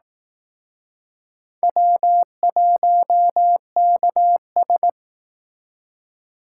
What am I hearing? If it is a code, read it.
Morse code, heard as a keyed tone, about 18 wpm, 700 Hz. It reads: W1KS
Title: Speed Racing - USA 1x2 Call Signs 15wpm